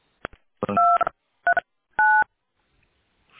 DTMF, heard as telephone touch tones, tones of about 260 ms. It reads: 239